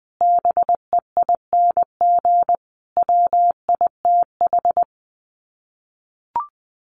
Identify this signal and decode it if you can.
Morse code, heard as a keyed tone, about 20 wpm, 700 Hz. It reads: BEING WITH
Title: Sets of 5 Words 20wpm